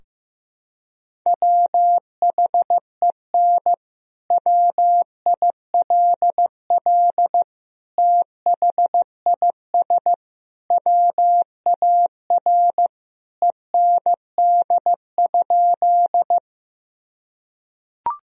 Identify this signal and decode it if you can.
Morse code, heard as a keyed tone, about 15 wpm, 700 Hz. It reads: WHEN WILL THIS WAR END?
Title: Sentences from Top 1000 Words - Review Part 2 - 15wpm